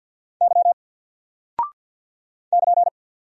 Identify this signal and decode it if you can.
Morse code, heard as a keyed tone, about 50 wpm, 700 Hz. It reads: Y <KN>
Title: Single Character (Letter:Number:Punctuation:Prosign) 50wpm